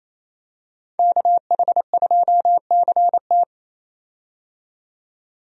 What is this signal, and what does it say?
Morse code, heard as a keyed tone, about 27 wpm, 700 Hz. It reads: KH2CT
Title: USA Call Signs - 5 Characters - 28wpm